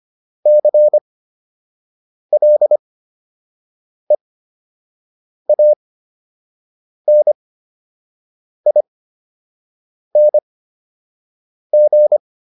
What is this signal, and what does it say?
Morse code, heard as a keyed tone, about 25 wpm, spaced as for 6 wpm, 600 Hz. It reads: CLEANING